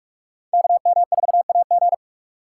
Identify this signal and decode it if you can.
Morse code, heard as a keyed tone, about 42 wpm, 700 Hz. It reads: KM4AG
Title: USA Call Signs - 5 Characters - 2x Character Spacing - 45wpm